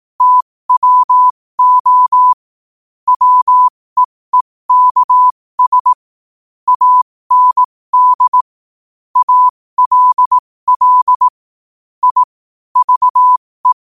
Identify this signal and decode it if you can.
Morse code, heard as a keyed tone, about 18 wpm, spaced as for 15 wpm, 1000 Hz. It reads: TWO WEEKS AND ALL I VE
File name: qotdfiles / 15wpm